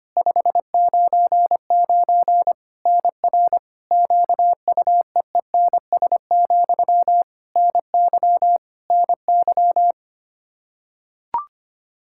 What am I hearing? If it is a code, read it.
Morse code, heard as a keyed tone, about 25 wpm, 700 Hz. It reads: 599 NR QUEENS, NY NY
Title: K1USN SST QSO Practice 25wpm